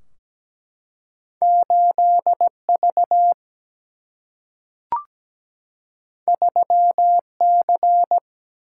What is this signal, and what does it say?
Morse code, heard as a keyed tone, about 17 wpm, 700 Hz. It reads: 8V 3C
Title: Single-Letter Single-Number 17wpm